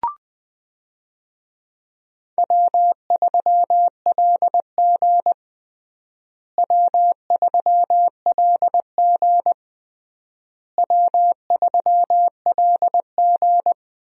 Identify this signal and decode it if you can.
Morse code, heard as a keyed tone, about 20 wpm, 700 Hz. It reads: W3LG W3LG W3LG
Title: USA Call Signs - 4 Characters - Repeated 3x - 20wpm